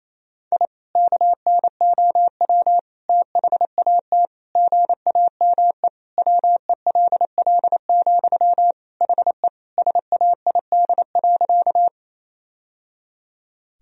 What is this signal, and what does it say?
Morse code, heard as a keyed tone, about 28 wpm, 700 Hz. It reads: I KNOW THAT GAME WELL, HE SAID.
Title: Sentences from Top 1000 Words - Review Part 3 - 28wpm